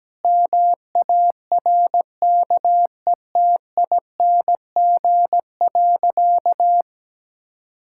Morse code, heard as a keyed tone, about 17 wpm, 700 Hz. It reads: MARKETING.
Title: Sentences from Top 1000 Words - Review Part 5 - 17wpm